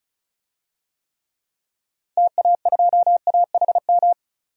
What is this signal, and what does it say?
Morse code, heard as a keyed tone, about 35 wpm, 700 Hz. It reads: TA2AHM
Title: DXCC-100 Call Signs - 6 Characters - 35wpm